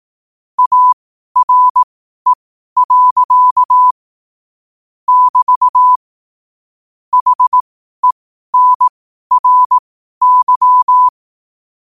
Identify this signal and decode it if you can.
Morse code, heard as a keyed tone, about 18 wpm, spaced as for 12 wpm, 1000 Hz. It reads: ARE. <BT> HENRY